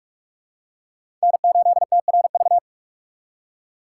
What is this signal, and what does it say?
Morse code, heard as a keyed tone, about 44 wpm, 700 Hz. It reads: N8TRV
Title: USA Call Signs - 5 Characters - 4x Character Spacing - 45wpm